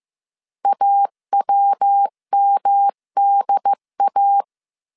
Morse code, heard as a keyed tone, about 15 wpm, 800 Hz. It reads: AWMDA
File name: CW / DET / 1 3 1500C S A 14wpm 2447